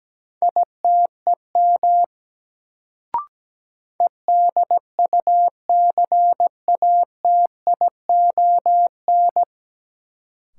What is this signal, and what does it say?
Morse code, heard as a keyed tone, about 17 wpm, 700 Hz. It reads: ITEM EDUCATION
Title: Sentences from Top 1000 Words - Review Part 6 - New Word Drill - 17wpm